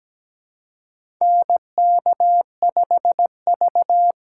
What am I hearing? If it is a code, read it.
Morse code, heard as a keyed tone, about 17 wpm, 700 Hz. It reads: NK5V
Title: USA Call Signs - 3x Character Spacing - 17wpm